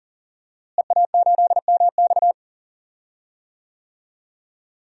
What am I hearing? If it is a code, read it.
Morse code, heard as a keyed tone, about 37 wpm, 700 Hz. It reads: EA8MX
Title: DXCC-100 Call Signs - 5 Characters - Repeated 3x - 40wpm